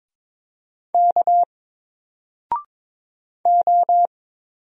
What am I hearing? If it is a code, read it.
Morse code, heard as a keyed tone, about 22 wpm, 700 Hz. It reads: K O
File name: Single-Letter-Number-22wpm